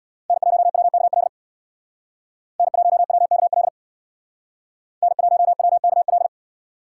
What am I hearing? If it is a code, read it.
Morse code, heard as a keyed tone, about 62 wpm, 700 Hz. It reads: N1WKL N1WKL N1WKL
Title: Speed Racing - USA 1x3 Call Signs 50wpm